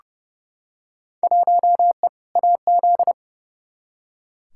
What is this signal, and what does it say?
Morse code, heard as a keyed tone, about 30 wpm, 700 Hz. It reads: 1E AZ
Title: Field Day Exchanges - 30wpm